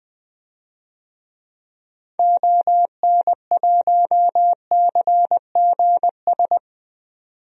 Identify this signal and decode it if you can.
Morse code, heard as a keyed tone, about 21 wpm, 700 Hz. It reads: ON1CGS
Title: DXCC-100 Call Signs - 6 Characters - 20wpm